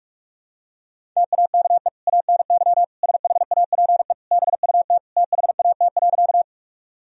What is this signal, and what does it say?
Morse code, heard as a keyed tone, about 46 wpm, 700 Hz. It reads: TAKE ANY SHAPE BUT THAT.
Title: Sentences from Top 1000 Words - Review Part 3 - 45wpm